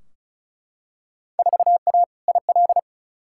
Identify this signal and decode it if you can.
Morse code, heard as a keyed tone, about 36 wpm, 700 Hz. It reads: 4A IL
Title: Field Day Exchanges - 35wpm